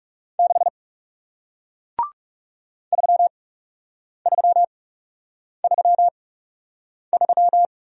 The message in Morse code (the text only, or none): 6 3333